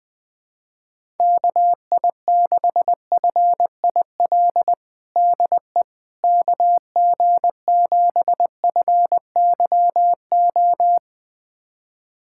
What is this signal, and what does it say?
Morse code, heard as a keyed tone, about 20 wpm, 700 Hz. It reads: KI6FIL DE KG7FYO
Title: USA Callsign DE Callsign 20wpm